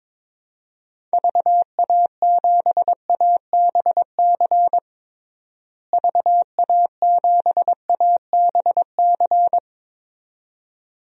Morse code, heard as a keyed tone, about 22 wpm, 700 Hz. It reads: VA7ABC VA7ABC
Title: DXCC-100 Call Signs - 6 Characters - Repeated 2x - 22wpm